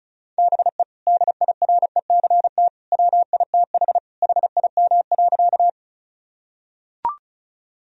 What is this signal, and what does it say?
Morse code, heard as a keyed tone, about 34 wpm, 700 Hz. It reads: BE DIRECT WITH HIM.
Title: Sentences from Top 1000 Words - Review Part 2 - 35wpm